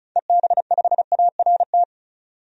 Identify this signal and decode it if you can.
Morse code, heard as a keyed tone, about 34 wpm, 700 Hz. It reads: EB5ART
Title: DXCC-100 Call Signs - 6 Characters - 35wpm